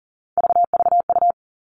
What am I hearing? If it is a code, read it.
Morse code, heard as a keyed tone, about 40 wpm, 700 Hz. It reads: VVU
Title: All Cut Numbers - 3-Digit 40wpm